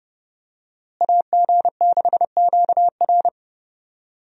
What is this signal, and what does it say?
Morse code, heard as a keyed tone, about 30 wpm, 700 Hz. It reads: AG6QR
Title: USA Call Signs - 5 Characters - Repeated 3x - 30wpm